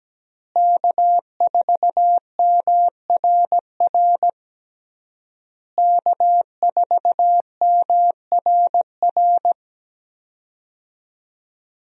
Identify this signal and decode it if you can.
Morse code, heard as a keyed tone, about 17 wpm, 700 Hz. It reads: K4MRR K4MRR
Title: USA Call Signs - 5 Characters - Repeated 3x - 17wpm